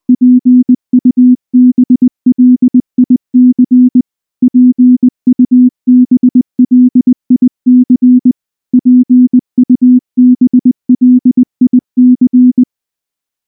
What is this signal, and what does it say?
Morse code, heard as a keyed tone, about 20 wpm, 250 Hz. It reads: PUBLIC PUBLIC PUBLIC
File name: morse / morse-training-20WPM-7